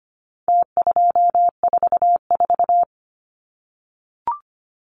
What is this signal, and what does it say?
Morse code, heard as a keyed tone, about 25 wpm, 700 Hz. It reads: T244
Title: Numbers - Contest Style - 4-Digit 25wpm